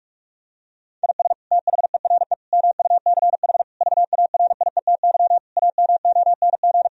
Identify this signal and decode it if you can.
Morse code, heard as a keyed tone, about 41 wpm, 700 Hz. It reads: IS THERE MUCH VARIETY AMONG
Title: Sentences from Top 1000 Words - Review Part 4 - 45wpm